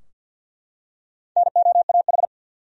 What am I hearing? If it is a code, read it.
Morse code, heard as a keyed tone, about 51 wpm, 700 Hz. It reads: NOAH
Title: USA Common Names - 50wpm